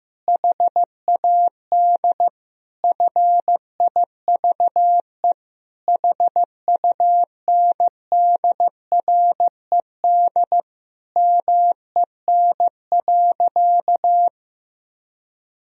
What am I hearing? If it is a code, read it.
Morse code, heard as a keyed tone, about 15 wpm, 700 Hz. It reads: HAD FIVE HUNDRED MEN.
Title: Sentences from Top 1000 Words - Review Part 2 - 15wpm